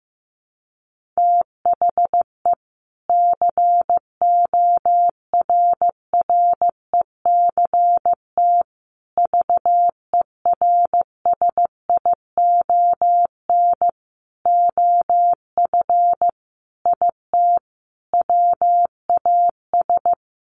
Morse code, heard as a keyed tone, about 15 wpm, 700 Hz. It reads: THE CORRECT VERSION OF IT WAS
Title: Sentences from Top 1400 Words 15wpm